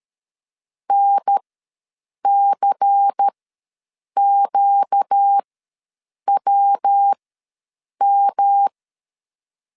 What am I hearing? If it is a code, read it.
Morse code, heard as a keyed tone, about 13 wpm, spaced as for 7 wpm, 800 Hz. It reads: NCQWM
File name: LEZIONI / 5 LBVC / 5 LBVC ws 12wpm 3159